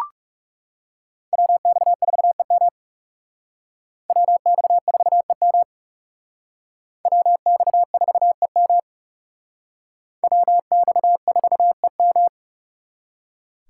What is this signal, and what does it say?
Morse code, heard as a keyed tone, about 38 wpm, 700 Hz. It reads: WX4EM WX4EM WX4EM WX4EM
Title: Speed Racing - USA 2x2 Call Signs 30wpm